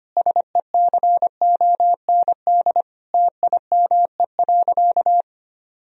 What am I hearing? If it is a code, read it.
Morse code, heard as a keyed tone, about 25 wpm, 700 Hz. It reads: SECOND TIME.
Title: Sentences from Top 1000 Words - Review Part 1 - 25wpm